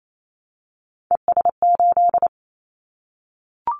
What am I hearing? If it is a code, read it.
Morse code, heard as a keyed tone, about 28 wpm, 700 Hz. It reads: ES8
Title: Callsign Prefixes for Top 100 DXCC - 28wpm